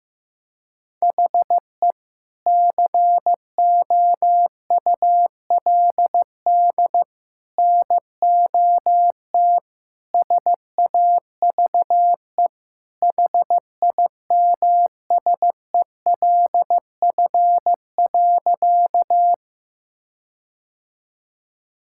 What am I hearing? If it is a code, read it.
Morse code, heard as a keyed tone, about 15 wpm, 700 Hz. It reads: HE COULD NOT SAVE HIMSELF.
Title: Sentences from Top 1600 Words 15wpm